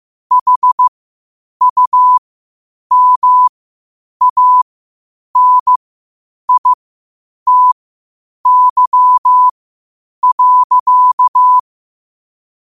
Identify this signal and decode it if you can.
Morse code, heard as a keyed tone, about 15 wpm, spaced as for 8 wpm, 1000 Hz. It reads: HUMANITY.
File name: qotdfiles / 8wpm